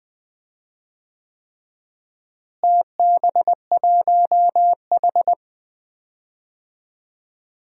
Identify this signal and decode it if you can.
Morse code, heard as a keyed tone, about 20 wpm, 700 Hz. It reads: TB1H
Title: DXCC-100 Call Signs - 4 Characters - Repeated 3x - 20wpm